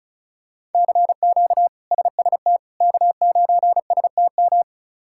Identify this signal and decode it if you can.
Morse code, heard as a keyed tone, about 34 wpm, 700 Hz. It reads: CQ SST K9STM
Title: K1USN SST QSO Practice - 35wpm